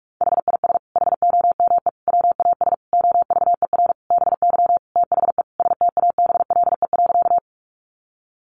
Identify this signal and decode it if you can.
Morse code, heard as a keyed tone, about 44 wpm, 700 Hz. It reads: HIS HOME WAS OVER BY THE STABLE.
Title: Sentences from Top 1500 Words 45wpm